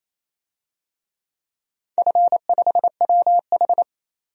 Morse code, heard as a keyed tone, about 28 wpm, 700 Hz. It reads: F5WH
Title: DXCC-100 Call Signs - 4 Characters - 28wpm